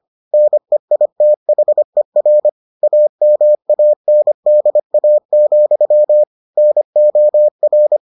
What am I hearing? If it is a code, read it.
Morse code, heard as a keyed tone, about 25 wpm, 600 Hz. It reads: NEITHER AMANDA, NOR